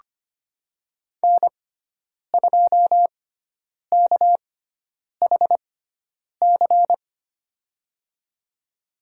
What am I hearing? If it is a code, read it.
Morse code, heard as a keyed tone, about 25 wpm, spaced as for 9 wpm, 700 Hz. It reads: N2KHC
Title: USA Call Signs - 5 Characters - 6x Character Spacing - 25wpm